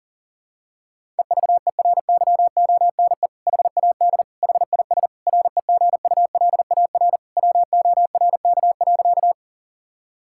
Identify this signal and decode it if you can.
Morse code, heard as a keyed tone, about 40 wpm, 700 Hz. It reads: EVERYONE HAD HIS REGULAR WORK.
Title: Sentences from Top 1000 Words - Review Part 6 - 40wpm